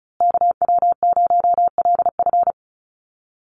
Morse code, heard as a keyed tone, about 34 wpm, 700 Hz. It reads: KW0LF
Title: USA Call Signs Repeated 3x 35wpm